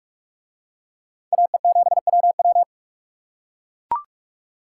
Morse code, heard as a keyed tone, about 42 wpm, 700 Hz. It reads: AE7WW